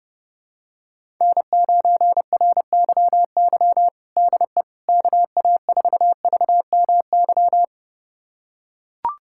Morse code, heard as a keyed tone, about 29 wpm, 700 Hz. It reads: N9RYY DE KA4VMY
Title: USA Callsign DE Callsign 30wpm